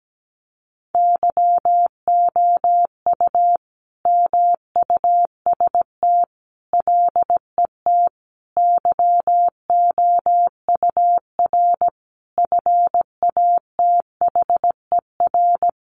Morse code, heard as a keyed tone, about 17 wpm, 700 Hz. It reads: YOU MUST LET YOUR FATHER
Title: Sentences from Top 1500 Words 17wpm